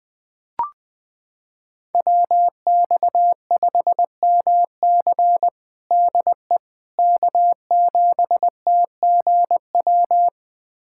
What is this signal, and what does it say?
Morse code, heard as a keyed tone, about 20 wpm, 700 Hz. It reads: WX5MC DE K7TGW
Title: USA Callsign DE Callsign - 5 Characters - 20wpm